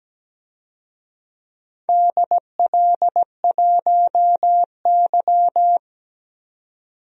Morse code, heard as a keyed tone, about 17 wpm, 700 Hz. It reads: DL1Y
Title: DXCC-100 Call Signs - 4 Characters - 17wpm